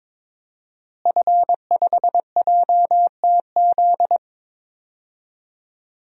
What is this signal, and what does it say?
Morse code, heard as a keyed tone, about 22 wpm, 700 Hz. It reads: F5JTZ
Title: DXCC-100 Call Signs - 5 Characters - Repeated 2x - 22wpm